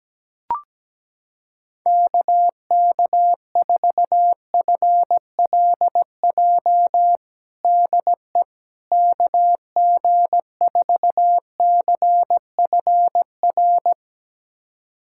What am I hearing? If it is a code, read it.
Morse code, heard as a keyed tone, about 17 wpm, 700 Hz. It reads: KK4FLJ DE KG4CFR
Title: USA Callsign DE Callsign - 6 Characters - 17wpm